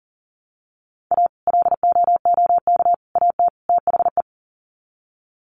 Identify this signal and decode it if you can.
Morse code, heard as a keyed tone, about 40 wpm, 700 Hz. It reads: A LOOK AT THE
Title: Sets of 4 Words 40wpm